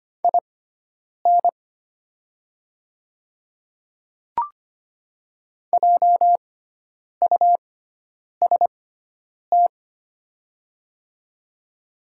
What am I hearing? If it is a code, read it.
Morse code, heard as a keyed tone, about 25 wpm, spaced as for 9 wpm, 700 Hz. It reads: IN JUST
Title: 6x Character Spacing - Top 200 Words - 25wpm